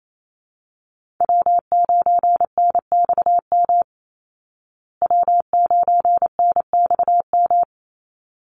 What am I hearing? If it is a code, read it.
Morse code, heard as a keyed tone, about 28 wpm, 700 Hz. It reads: W9NXM W9NXM
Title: Call Signs Repeated 3x - 28wpm